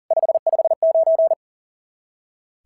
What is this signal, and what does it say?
Morse code, heard as a keyed tone, about 40 wpm, 690 Hz. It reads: 559
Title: SOTA QSO Practice 40wpm